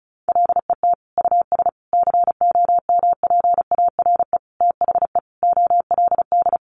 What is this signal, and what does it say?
Morse code, heard as a keyed tone, about 34 wpm, 700 Hz. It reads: LET US COMPARE THE OLD
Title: Sentences from Top 1300 Words 35wpm